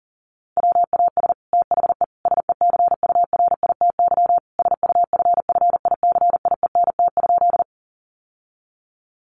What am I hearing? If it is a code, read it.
Morse code, heard as a keyed tone, about 40 wpm, 700 Hz. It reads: WAS THE SECURITY SUFFICIENT?
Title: Sentences from Top 1400 Words 40wpm